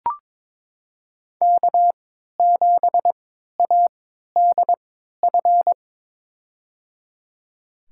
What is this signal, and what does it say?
Morse code, heard as a keyed tone, about 22 wpm, spaced as for 13 wpm, 700 Hz. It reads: K7ADF
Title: USA Call Signs - 5 Characters - 3x Character Spacing - 22wpm